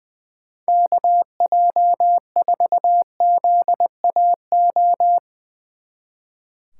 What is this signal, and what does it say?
Morse code, heard as a keyed tone, about 21 wpm, 700 Hz. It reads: KJ4ZAO
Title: Speed Racing - USA 2x3 Call Signs 20wpm